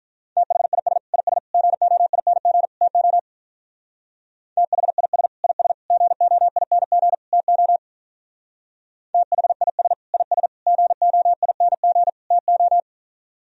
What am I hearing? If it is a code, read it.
Morse code, heard as a keyed tone, about 45 wpm, 700 Hz. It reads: THIS IS GOING TO THIS IS GOING TO THIS IS GOING TO